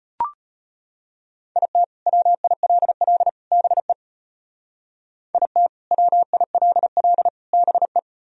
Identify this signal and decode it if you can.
Morse code, heard as a keyed tone, about 32 wpm, 700 Hz. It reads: IT WILL BE IT WILL BE